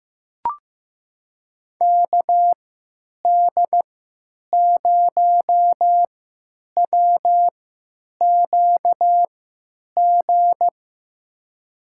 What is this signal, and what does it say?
Morse code, heard as a keyed tone, about 15 wpm, spaced as for 8 wpm, 700 Hz. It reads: KD0WQG
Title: USA Call Signs - 3x Character Spacing - 15wpm